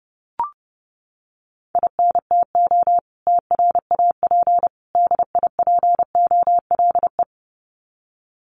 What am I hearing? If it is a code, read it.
Morse code, heard as a keyed tone, about 30 wpm, 700 Hz. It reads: INTO TRAP DIPOLE
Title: Common QSO Phrases 30wpm